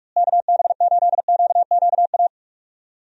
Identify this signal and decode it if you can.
Morse code, heard as a keyed tone, about 43 wpm, 700 Hz. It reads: KB8QQA
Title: USA Call Signs - 6 Characters - Repeated 2x - 45wpm